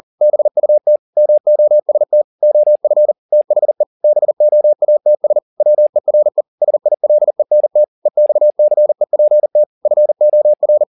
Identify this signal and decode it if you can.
Morse code, heard as a keyed tone, about 40 wpm, 600 Hz. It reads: BUT MOST OF THE BOATS WERE SILENT EXCEPT FOR